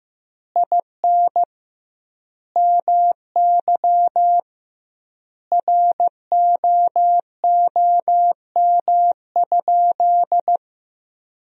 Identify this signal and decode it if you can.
Morse code, heard as a keyed tone, about 15 wpm, 700 Hz. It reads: IN MY ROOM?
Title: 2x Word-Spacing - Sentences from Top 300 Words - 15wpm